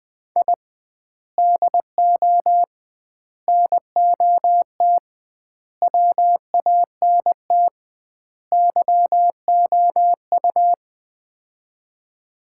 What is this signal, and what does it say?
Morse code, heard as a keyed tone, about 20 wpm, 700 Hz. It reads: I DO NOT WANT YOU
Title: 2x Word-Spacing - Sets of 5 Words - 20wpm